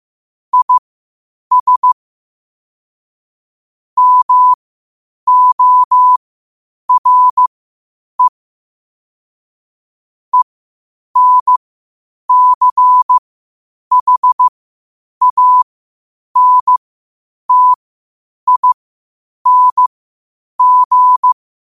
Morse code, heard as a keyed tone, about 15 wpm, spaced as for 8 wpm, 1000 Hz. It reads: IS MORE ENCHANTING
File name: qotdfiles / 8wpm